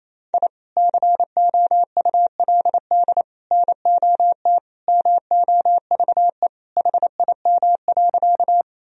Morse code, heard as a keyed tone, about 28 wpm, 700 Hz. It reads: I COULD NOT MOVE HIM.
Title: Sentences from Top 1000 Words - Review Part 1 - 28wpm